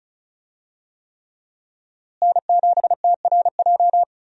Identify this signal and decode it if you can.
Morse code, heard as a keyed tone, about 36 wpm, 700 Hz. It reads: N7TRJ
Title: USA Call Signs - 5 Characters - Repeated 3x - 35wpm